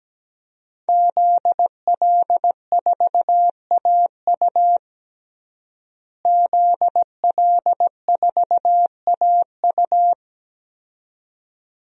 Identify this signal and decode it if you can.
Morse code, heard as a keyed tone, about 17 wpm, 700 Hz. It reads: ZL4AU ZL4AU
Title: DXCC-100 Call Signs - 5 Characters - Repeated 2x - 17wpm